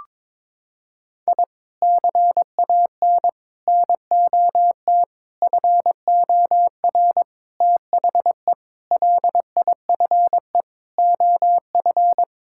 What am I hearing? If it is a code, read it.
Morse code, heard as a keyed tone, about 21 wpm, 700 Hz. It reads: I CAN NOT FOR THE LIFE OF
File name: practice / 22wpm / Sets of 8 Words - 22wpm